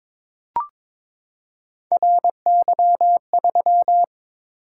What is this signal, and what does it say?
Morse code, heard as a keyed tone, about 22 wpm, 700 Hz. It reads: RY3
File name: practice / 22wpm / Three Letter-Number Combinations - 22wpm